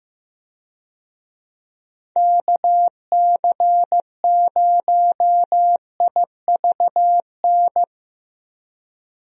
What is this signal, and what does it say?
Morse code, heard as a keyed tone, about 15 wpm, 700 Hz. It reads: KC0IVN
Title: USA Call Signs - 6 Characters - Repeated 2x - 15wpm